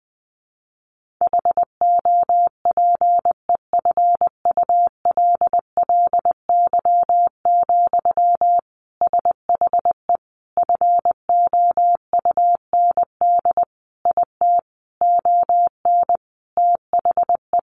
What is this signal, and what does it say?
Morse code, heard as a keyed tone, about 20 wpm, 700 Hz. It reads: HOPEFULLY, SHE FOUND IT ON THE
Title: Sentences from Top 1500 Words 20wpm